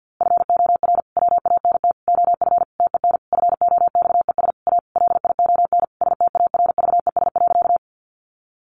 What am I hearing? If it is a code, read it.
Morse code, heard as a keyed tone, about 50 wpm, 700 Hz. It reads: FOR WANT OF TEN FOXES A LION STARVES.
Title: A Sound Of Thunder 50wpm